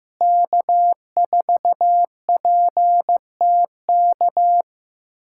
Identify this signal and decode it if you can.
Morse code, heard as a keyed tone, about 15 wpm, 700 Hz. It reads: K4PTK
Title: USA Callsign DE Callsign - 5 Characters - 15wpm